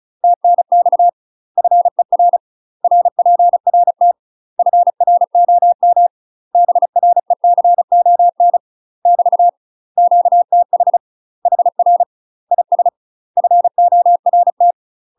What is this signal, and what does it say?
Morse code, heard as a keyed tone, about 34 wpm, 700 Hz. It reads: TNX FER RPRT FROM BRECON <BT> QTH HR IS FORT
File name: UK MASTER QSOs TRAINING/35WPM